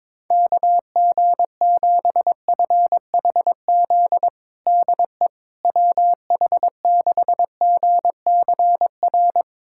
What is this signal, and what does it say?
Morse code, heard as a keyed tone, about 22 wpm, 700 Hz. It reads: KG7FHZ DE WH6GCR
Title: USA Callsign DE Callsign - 6 Characters - 22wpm